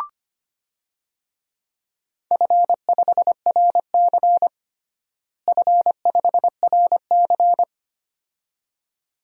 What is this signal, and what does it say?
Morse code, heard as a keyed tone, about 25 wpm, 700 Hz. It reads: F5RC F5RC
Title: DXCC-100 Call Signs - 4 Characters - Repeated 2x - 25wpm